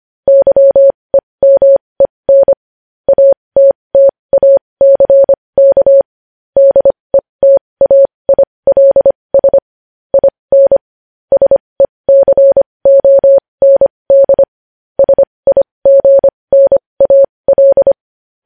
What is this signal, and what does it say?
Morse code, heard as a keyed tone, about 25 wpm, spaced as for 20 wpm, 550 Hz. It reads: YEMEN ATTACK DETAILS IN SECOND SIGNAL
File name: podcast / News headlines-1x-25x20